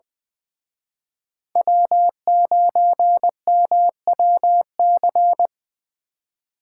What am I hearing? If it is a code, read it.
Morse code, heard as a keyed tone, about 23 wpm, 700 Hz. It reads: W9MWC
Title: Speed Racing - USA 1x3 Call Signs 20wpm